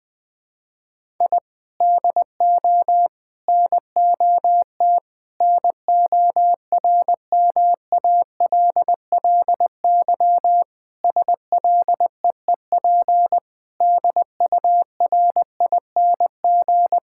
Morse code, heard as a keyed tone, about 20 wpm, 700 Hz. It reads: I DO NOT NORMALLY SLEEP DURING
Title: Sentences from Top 1000 Words - Review Part 6 - 20wpm